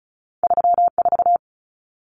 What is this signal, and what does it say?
Morse code, heard as a keyed tone, about 36 wpm, 700 Hz. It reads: 34
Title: Common Cut Numbers - 2-Digit 35wpm